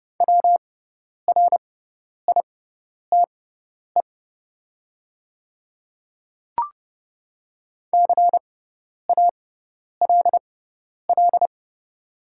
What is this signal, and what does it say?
Morse code, heard as a keyed tone, about 30 wpm, spaced as for 10 wpm, 700 Hz. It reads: WRITE CALL
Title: 6x Character Spacing - Top 100 Words - 30wpm